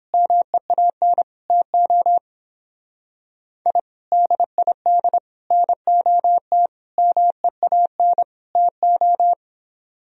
Morse code, heard as a keyed tone, about 29 wpm, 700 Hz. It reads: MEAN TO I DID NOT MEAN TO